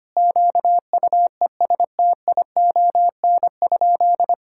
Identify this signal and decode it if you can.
Morse code, heard as a keyed tone, about 25 wpm, 700 Hz. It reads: QUESTION?
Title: Sentences from Top 1000 Words - Review Part 5 - 25wpm